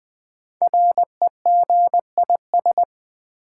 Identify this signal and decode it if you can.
Morse code, heard as a keyed tone, about 20 wpm, 700 Hz. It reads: REGIS
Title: Build A Word - 5 Letters At A Time - 12-Letter Words - 20wpm